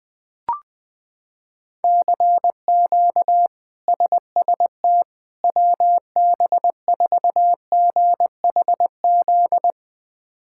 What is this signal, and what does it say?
Morse code, heard as a keyed tone, about 20 wpm, 700 Hz. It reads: CQ SST WB4GHZ
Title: K1USN SST QSO Practice - 20wpm